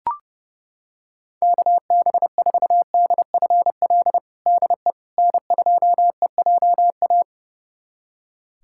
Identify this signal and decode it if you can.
Morse code, heard as a keyed tone, about 32 wpm, 700 Hz. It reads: KB4DFL DE N2EJA
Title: USA Callsign DE Callsign 30wpm